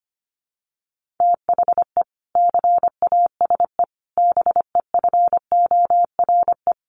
Morse code, heard as a keyed tone, about 25 wpm, 700 Hz. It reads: THE CASE BEFORE
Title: Sentences from Top 1400 Words 25wpm